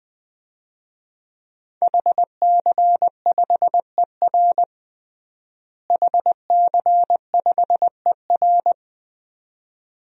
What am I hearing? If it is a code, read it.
Morse code, heard as a keyed tone, about 21 wpm, 700 Hz. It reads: HC5ER HC5ER
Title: DXCC-100 Call Signs - 5 Characters - Repeated 2x - 20wpm